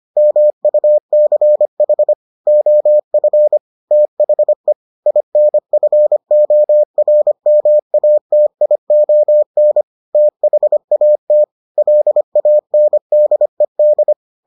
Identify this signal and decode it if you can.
Morse code, heard as a keyed tone, about 25 wpm, 600 Hz. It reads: MUCH OF THE INFORMATION THAT LANDED